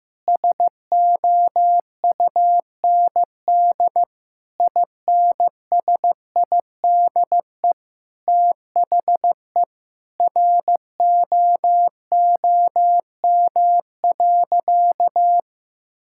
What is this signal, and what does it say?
Morse code, heard as a keyed tone, about 15 wpm, 700 Hz. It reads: SOUND INSIDE THE ROOM.
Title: Sentences from Top 1000 Words - Review Part 6 - 15wpm